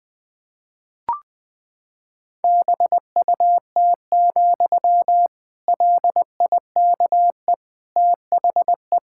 Morse code, heard as a keyed tone, about 20 wpm, 700 Hz. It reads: BUT, LIKE THE
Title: Sentences from Top 1600 Words 20wpm